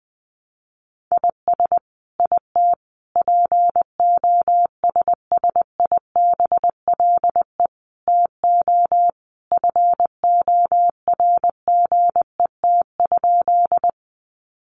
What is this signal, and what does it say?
Morse code, heard as a keyed tone, about 20 wpm, 700 Hz. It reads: IS IT POSSIBLE TO FORGET?
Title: Sentences from Top 1200 Words 20wpm